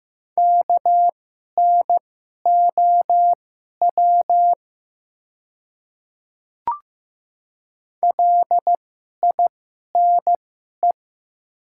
Morse code, heard as a keyed tone, about 15 wpm, spaced as for 11 wpm, 700 Hz. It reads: KNOW LINE
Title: 2x Character Spacing - Top 200 Words - 15wpm